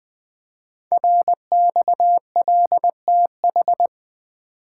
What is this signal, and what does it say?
Morse code, heard as a keyed tone, about 20 wpm, 700 Hz. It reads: RXLTH
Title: Code Groups - 5 Letters - 20wpm